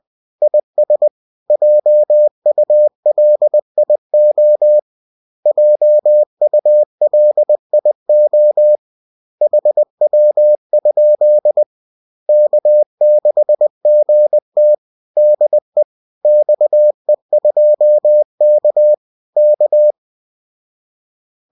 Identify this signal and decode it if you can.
Morse code, heard as a keyed tone, about 20 wpm, 600 Hz. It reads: IS JULIO JULIO HW? K6GT DE XE2K K